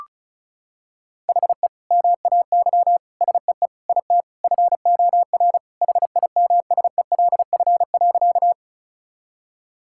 Morse code, heard as a keyed tone, about 36 wpm, 700 Hz. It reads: HE MAY SEE IT FOR HIMSELF.
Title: Sentences from Top 1600 Words 35wpm